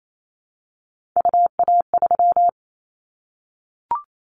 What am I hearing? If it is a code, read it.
Morse code, heard as a keyed tone, about 28 wpm, 700 Hz. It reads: UA3
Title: Callsign Prefixes for Top 25 DXCC - 28wpm